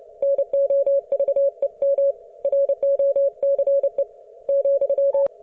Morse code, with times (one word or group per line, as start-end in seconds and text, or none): 0.2-2.1 NOVEM
2.4-4.0 ROCE
4.5-5.3 ,